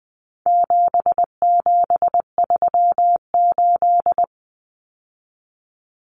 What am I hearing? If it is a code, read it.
Morse code, heard as a keyed tone, about 20 wpm, 700 Hz. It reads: 7738
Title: Numbers - 4-Digit 20wpm